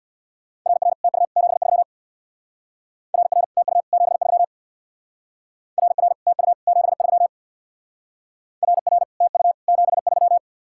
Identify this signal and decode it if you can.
Morse code, heard as a keyed tone, about 64 wpm, 700 Hz. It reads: RR TU 73 RR TU 73 RR TU 73 RR TU 73